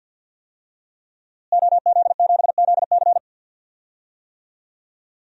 Morse code, heard as a keyed tone, about 50 wpm, 700 Hz. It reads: OZ7ZC
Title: DXCC-100 Call Signs - 5 Characters - Repeated 2x - 50wpm